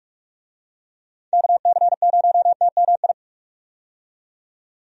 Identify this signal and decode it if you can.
Morse code, heard as a keyed tone, about 45 wpm, 700 Hz. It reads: KC0TMI
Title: USA Call Signs - 6 Characters - 45wpm